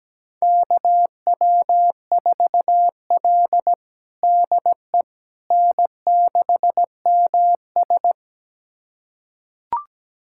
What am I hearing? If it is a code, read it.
Morse code, heard as a keyed tone, about 17 wpm, 700 Hz. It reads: KW4L DE N6MS
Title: USA Callsign DE Callsign - 4 Characters - 17wpm